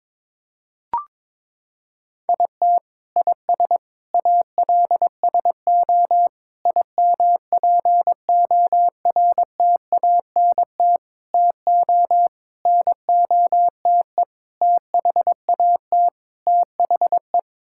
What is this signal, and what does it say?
Morse code, heard as a keyed tone, about 21 wpm, 700 Hz. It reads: IT IS ALSO IMPORTANT TO NOTE THAT THE
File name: practice / 22wpm / Sets of 8 Words - 22wpm